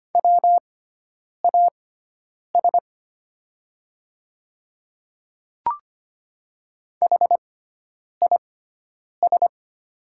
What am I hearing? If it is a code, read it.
Morse code, heard as a keyed tone, about 25 wpm, spaced as for 9 wpm, 700 Hz. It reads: WAS HIS
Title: 6x Character Spacing - Top 100 Words - 25wpm